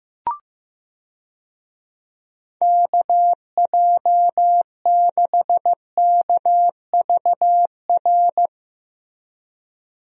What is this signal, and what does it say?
Morse code, heard as a keyed tone, about 15 wpm, 700 Hz. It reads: KJ6KVR
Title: USA Call Signs - 6 Characters - Repeated 2x - 15wpm